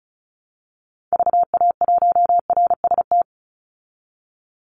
Morse code, heard as a keyed tone, about 34 wpm, 700 Hz. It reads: VA1RST
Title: Call Signs Repeated 2x - 35wpm